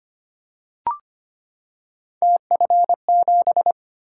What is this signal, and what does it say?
Morse code, heard as a keyed tone, about 25 wpm, 700 Hz. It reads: TF7
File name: Three Letter-Number Combinations - 25wpm